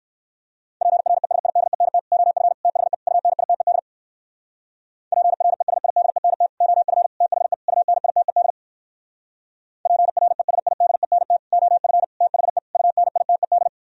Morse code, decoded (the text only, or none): PRESIDENT OF THE UNITED PRESIDENT OF THE UNITED PRESIDENT OF THE UNITED